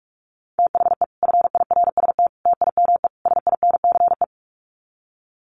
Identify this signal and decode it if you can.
Morse code, heard as a keyed tone, about 44 wpm, 700 Hz. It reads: THE FIRST TIME SINCE
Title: Sets of 4 Words 45wpm